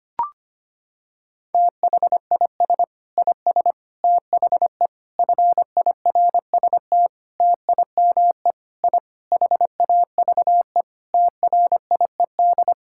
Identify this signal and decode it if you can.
Morse code, heard as a keyed tone, about 25 wpm, 700 Hz. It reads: THIS IS THE FIRST TIME I HAVE TRIED
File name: practice / 25wpm / Sentences with 4-Word Phrases - 25wpm